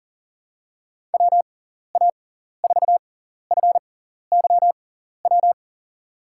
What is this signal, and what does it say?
Morse code, heard as a keyed tone, about 39 wpm, spaced as for 14 wpm, 700 Hz. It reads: WA4FYW
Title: USA Call Signs - 6 Characters - 6x Character Spacing - 40wpm